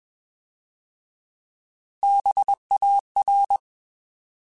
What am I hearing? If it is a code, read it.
Morse code, heard as a keyed tone, about 21 wpm, 800 Hz. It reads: BAR
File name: Alphabetical / Alphabetical mp3s / Bs @20